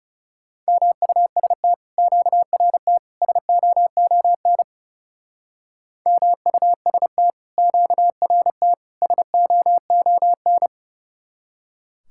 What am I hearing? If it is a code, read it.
Morse code, heard as a keyed tone, about 38 wpm, 700 Hz. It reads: MUST QRT SOON MUST QRT SOON